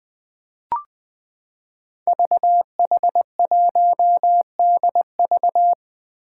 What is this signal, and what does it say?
Morse code, heard as a keyed tone, about 20 wpm, 700 Hz. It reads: VH1DV
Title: Five Challenging Letter-Number Combinations 20wpm